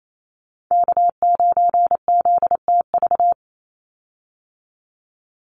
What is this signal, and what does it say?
Morse code, heard as a keyed tone, about 28 wpm, 700 Hz. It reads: K9ZTV
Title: Call Signs Repeated 2x - 28wpm